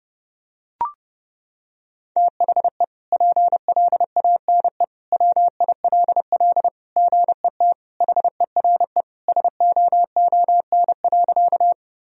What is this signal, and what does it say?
Morse code, heard as a keyed tone, about 30 wpm, 700 Hz. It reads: THE PLANE WILL GET HERE SOON.
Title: Sentences from Top 1000 Words - Review Part 3 - 30wpm